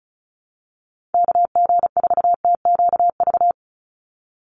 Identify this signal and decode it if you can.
Morse code, heard as a keyed tone, about 34 wpm, 700 Hz. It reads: KG4TQV
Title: USA Call Signs 35wpm